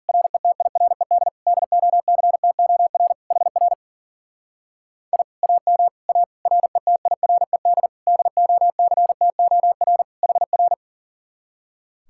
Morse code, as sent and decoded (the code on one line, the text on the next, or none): .-. - .. .-. -.. / -.. --- -.-. - --- .-. / .... .-. / .. / .- -- / .- / .-. - .. .-. -.. / -.. --- -.-. - --- .-. / .... .-.
RETIRED DOCTOR HR I AM A RETIRED DOCTOR HR